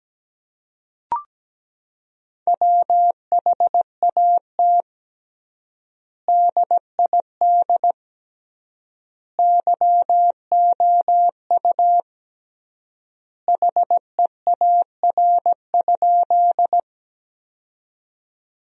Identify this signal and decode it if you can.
Morse code, heard as a keyed tone, about 17 wpm, 700 Hz. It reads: WHAT DID YOU HEAR?
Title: 3x Word-Spacing - Sentences from Top 300 Words - 17wpm